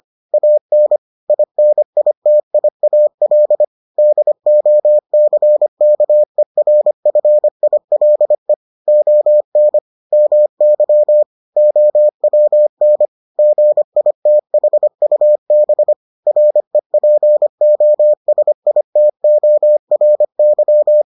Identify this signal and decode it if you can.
Morse code, heard as a keyed tone, about 25 wpm, 600 Hz. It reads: AN INITIAL DOCKERFILE ON MY OWN GITHUB REPOSITORY